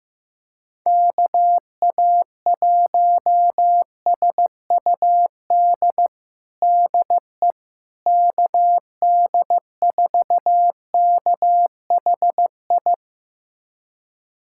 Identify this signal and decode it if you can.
Morse code, heard as a keyed tone, about 16 wpm, 700 Hz. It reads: KA1SUD DE KD4KHI
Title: USA Callsign DE Callsign - 6 Characters - 15wpm